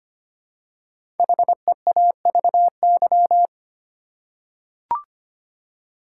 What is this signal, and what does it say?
Morse code, heard as a keyed tone, about 25 wpm, 700 Hz. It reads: HEAVY
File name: 5 Letter Words - First 3 Letters Given - 25wpm